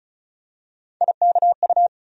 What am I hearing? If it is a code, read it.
Morse code, heard as a keyed tone, about 34 wpm, 700 Hz. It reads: IKU
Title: USA Call Sign Postfixes - 3 Letters 35wpm